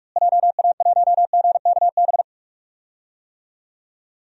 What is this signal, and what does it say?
Morse code, heard as a keyed tone, about 46 wpm, 700 Hz. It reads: JA1GKB
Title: DXCC-100 Call Signs - 6 Characters - Repeated 2x - 45wpm